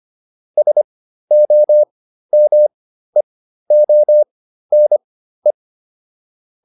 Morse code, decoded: SOMEONE

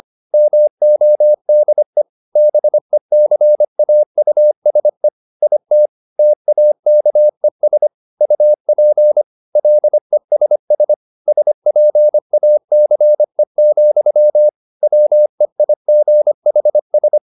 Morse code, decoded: MODE BECAUSE IT TAKES UP LESS SPACE, WEIGHS

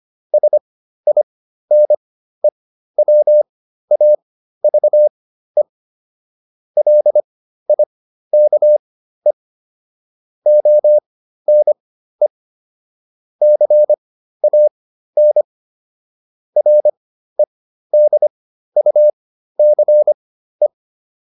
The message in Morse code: SINE WAVE LIKE ONE CAN REDUCE

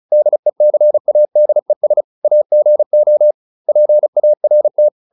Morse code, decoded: DECADES AGO PART

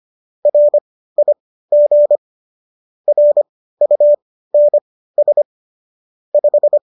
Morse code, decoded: RIG RUNS 5

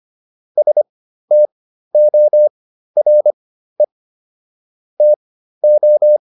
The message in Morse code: STORE TO

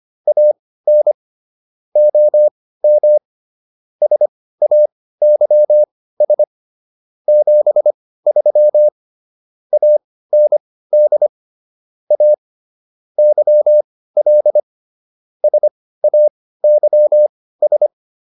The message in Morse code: AN OM SAYS 73 AND A YL SAYS

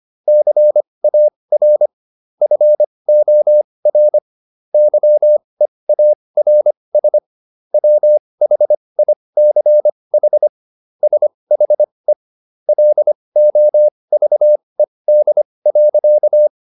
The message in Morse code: CAR FOR YEARS WHICH SHE LOVED.

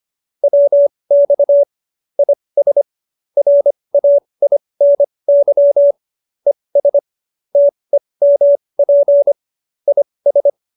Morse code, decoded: WX IS RAINY ES TEMP IS

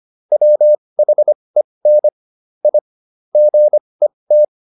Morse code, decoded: WHEN I GET

